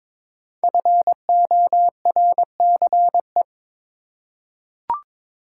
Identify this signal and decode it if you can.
Morse code, heard as a keyed tone, about 22 wpm, 700 Hz. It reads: FORCE
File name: 5 Letter Words - First 3 Letters Given - 22wpm